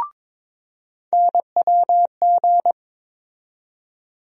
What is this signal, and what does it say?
Morse code, heard as a keyed tone, about 22 wpm, 700 Hz. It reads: NWG
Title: USA Call Sign Postfixes - 3 Letters 22wpm